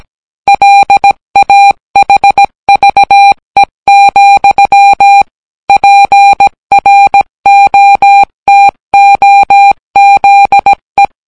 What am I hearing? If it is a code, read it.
Morse code, heard as a keyed tone, about 17 wpm, 800 Hz. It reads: LAHVE, PROTOZE